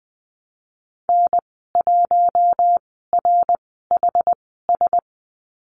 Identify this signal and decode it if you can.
Morse code, heard as a keyed tone, about 19 wpm, spaced as for 14 wpm, 700 Hz. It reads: N1RHS
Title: 076 Lesson 13g - Letter L - Call Signs 20wpm